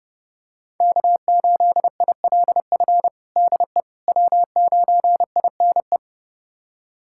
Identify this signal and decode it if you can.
Morse code, heard as a keyed tone, about 30 wpm, 700 Hz. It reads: K8ILF DE W9INE
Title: USA Callsign DE Callsign - 5 Characters - 30wpm